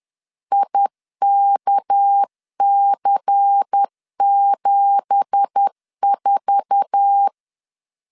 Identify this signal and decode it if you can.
Morse code, heard as a keyed tone, about 10 wpm, 800 Hz. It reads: IKC74
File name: LEZIONI / 9 2867 10wpm 2733